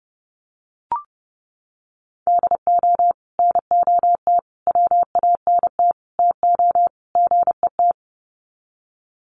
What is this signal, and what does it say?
Morse code, heard as a keyed tone, about 30 wpm, 700 Hz. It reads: DO NOT WANT TO GET
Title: Sets of 5 Words 30wpm